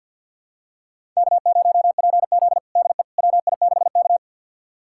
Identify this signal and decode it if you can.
Morse code, heard as a keyed tone, about 52 wpm, 700 Hz. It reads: K0PZ DE WI6K
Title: USA Callsign DE Callsign - 4 Characters - 50wpm